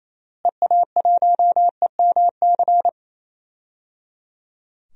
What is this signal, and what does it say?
Morse code, heard as a keyed tone, about 28 wpm, 700 Hz. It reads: EA1EMC
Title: DXCC-100 Call Signs - 6 Characters - Repeated 3x - 28wpm